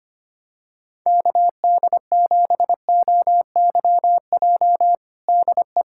Morse code, heard as a keyed tone, about 25 wpm, 700 Hz. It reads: KD7OYJ DE
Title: USA Callsign DE Callsign - 6 Characters - 25wpm